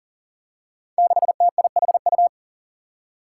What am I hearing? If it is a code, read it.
Morse code, heard as a keyed tone, about 40 wpm, 700 Hz. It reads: 6TIHU
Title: Five Challenging Letter-Number Combinations 40wpm